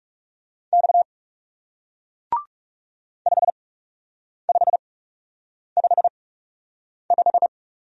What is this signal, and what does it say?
Morse code, heard as a keyed tone, about 38 wpm, spaced as for 8 wpm, 700 Hz. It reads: X 5555